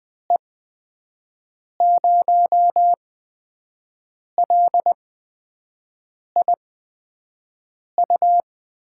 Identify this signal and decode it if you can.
Morse code, heard as a keyed tone, about 20 wpm, spaced as for 5 wpm, 700 Hz. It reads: E0LIU
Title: USA Call Signs - 8x Character Spacing - 20wpm